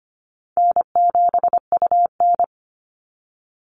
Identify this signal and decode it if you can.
Morse code, heard as a keyed tone, about 25 wpm, 700 Hz. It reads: N7UN
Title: NA SOTA Activator Callsigns 25wpm